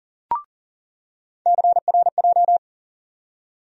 Morse code, heard as a keyed tone, about 40 wpm, 700 Hz. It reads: CRJ
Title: USA Call Sign Postfixes - 3 Letters 40wpm